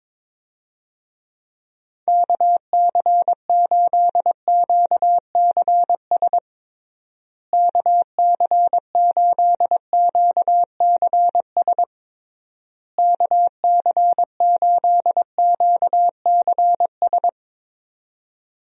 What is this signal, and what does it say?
Morse code, heard as a keyed tone, about 22 wpm, 700 Hz. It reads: KC8QCS KC8QCS KC8QCS
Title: USA Call Signs - 6 Characters - Repeated 3x - 22wpm